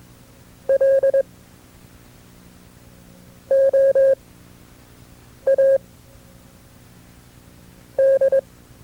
Morse code, heard as a keyed tone, about 22 wpm, spaced as for 4 wpm, 540 Hz. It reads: LOAD
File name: FreeFiles / CW-practice / KB6MT-tape2-side2